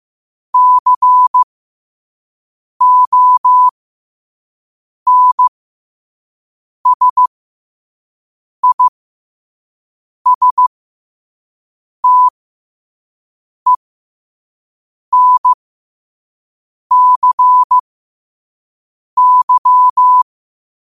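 Morse code, heard as a keyed tone, about 15 wpm, spaced as for 5 wpm, 1000 Hz. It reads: CONSISTENCY
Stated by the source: Morse code quote of the day at 5 WPM.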